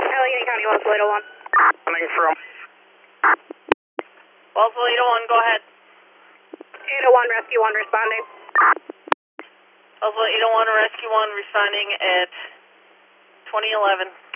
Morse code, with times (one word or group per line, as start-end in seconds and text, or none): none